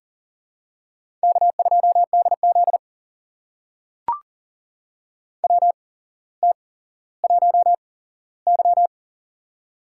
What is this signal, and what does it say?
Morse code, heard as a keyed tone, about 42 wpm, spaced as for 17 wpm, 700 Hz. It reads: K2DZ WT1Y
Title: USA Call Signs - 8x Character Spacing - 40wpm